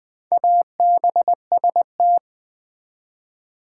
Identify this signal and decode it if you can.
Morse code, heard as a keyed tone, about 20 wpm, 700 Hz. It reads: ABST
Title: Build A Word - 4 Letters At A Time - 8-Letter Words - 20wpm